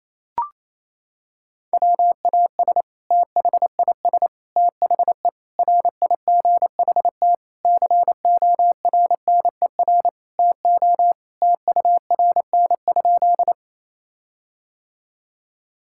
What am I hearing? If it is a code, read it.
Morse code, heard as a keyed tone, about 28 wpm, 700 Hz. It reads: WAS THIS THE RIGHT CORNER TO TURN?
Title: Sentences from Top 1600 Words 28wpm